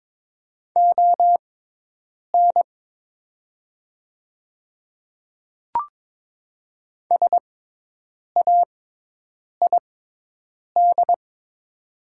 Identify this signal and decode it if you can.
Morse code, heard as a keyed tone, about 22 wpm, spaced as for 8 wpm, 700 Hz. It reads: ON SAID
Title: 6x Character Spacing - Top 100 Words - 22wpm